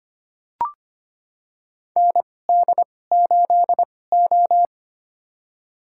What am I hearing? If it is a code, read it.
Morse code, heard as a keyed tone, about 25 wpm, spaced as for 18 wpm, 700 Hz. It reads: ND8O
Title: USA Call Signs - 2x Character Spacing - 25wpm